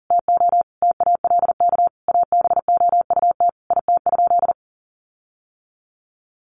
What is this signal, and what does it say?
Morse code, heard as a keyed tone, about 40 wpm, 700 Hz. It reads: TO TALK ABOUT IT?
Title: Sentences from Top 1400 Words 40wpm